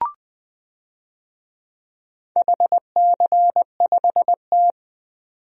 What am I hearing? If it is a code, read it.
Morse code, heard as a keyed tone, about 20 wpm, 700 Hz. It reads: HC5T
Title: DXCC-100 Call Signs - 4 Characters - 20wpm